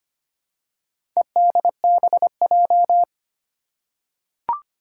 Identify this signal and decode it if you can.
Morse code, heard as a keyed tone, about 25 wpm, 700 Hz. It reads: EDBJ
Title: Four Challenging Letter-Number Combinations 25wpm